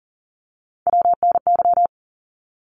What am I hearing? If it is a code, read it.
Morse code, heard as a keyed tone, about 40 wpm, 700 Hz. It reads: WNY
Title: ARRL Sections 40wpm